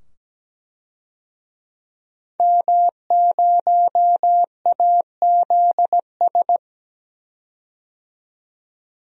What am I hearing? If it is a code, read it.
Morse code, heard as a keyed tone, about 17 wpm, 700 Hz. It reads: M0AZS
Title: DXCC-100 Call Signs - 5 Characters - 17wpm